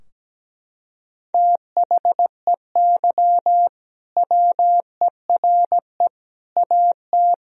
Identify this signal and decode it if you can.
Morse code, heard as a keyed tone, about 17 wpm, 700 Hz. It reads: THEY WERE AT
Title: Sentences from Top 1000 Words - Review Part 2 - 17wpm